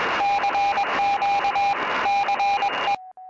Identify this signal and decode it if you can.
Morse code, heard as a keyed tone, about 21 wpm, 800 Hz, with noise. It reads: CQ CE